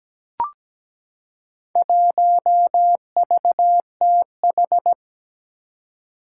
Morse code, heard as a keyed tone, about 17 wpm, 700 Hz. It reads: 1VTH
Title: Four Challenging Letter-Number Combinations 17wpm